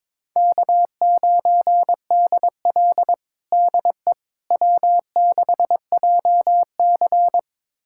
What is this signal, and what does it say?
Morse code, heard as a keyed tone, about 21 wpm, 700 Hz. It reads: K9DL DE W6JC
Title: USA Callsign DE Callsign - 4 Characters - 22wpm